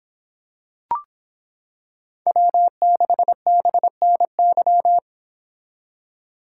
Morse code, heard as a keyed tone, about 26 wpm, 700 Hz. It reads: W6BNY